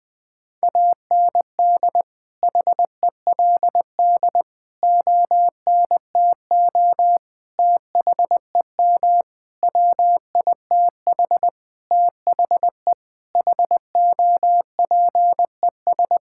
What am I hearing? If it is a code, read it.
Morse code, heard as a keyed tone, about 20 wpm, 700 Hz. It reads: AND HELD ONTO THEM WITH THE HOPES
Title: Sentences with 9-Word Phrases - 20wpm